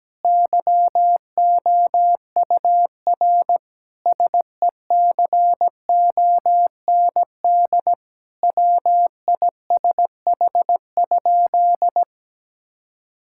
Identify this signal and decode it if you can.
Morse code, heard as a keyed tone, about 17 wpm, 700 Hz. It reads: YOUR SECOND WISH?
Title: Sentences from Top 1600 Words 17wpm